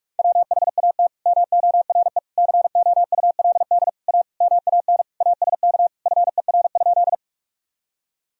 Morse code, heard as a keyed tone, about 46 wpm, 700 Hz. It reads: WHAT MORE COULD A MAN ASK FER?
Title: Sentences from Top 200 Words and Abbreviations 45wpm